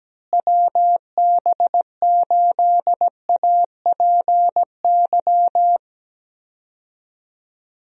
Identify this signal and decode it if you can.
Morse code, heard as a keyed tone, about 17 wpm, 700 Hz. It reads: WB8APY
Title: USA Call Signs - 6 Characters - Repeated 2x - 17wpm